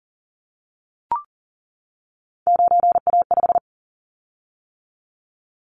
Morse code, heard as a keyed tone, about 39 wpm, 700 Hz. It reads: 9A5
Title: Callsign Prefixes for Top 50 DXCC - 40wpm